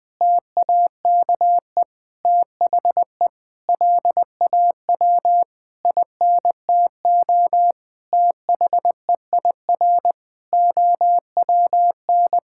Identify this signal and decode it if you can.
Morse code, heard as a keyed tone, about 20 wpm, 700 Hz. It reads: TAKE THE LAW INTO THEIR OWN
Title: Sentences with 8-Word Phrases - 20wpm